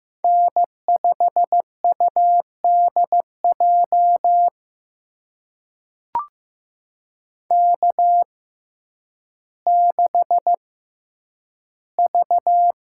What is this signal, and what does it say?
Morse code, heard as a keyed tone, about 15 wpm, spaced as for 9 wpm, 700 Hz. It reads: N5UDJ K6V
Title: USA Call Signs - 5 Characters - 6x Character Spacing - 15wpm